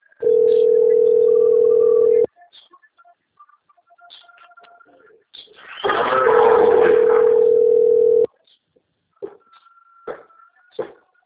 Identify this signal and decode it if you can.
Ringback tone (North America), 440 + 480 Hz.
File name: rg-1600-unknown- - - .47186